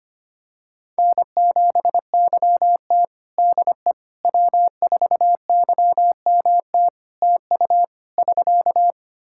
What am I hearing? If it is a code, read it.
Morse code, heard as a keyed tone, about 25 wpm, 700 Hz. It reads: N7YT DE W4YMT TU <SK>
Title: K1USN SST QSO Practice 25wpm